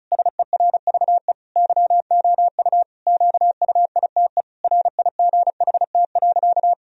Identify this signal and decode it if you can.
Morse code, heard as a keyed tone, about 34 wpm, 700 Hz. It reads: SERVE YOU QUITE RIGHT.
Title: Sentences from Top 1000 Words - Review Part 2 - 35wpm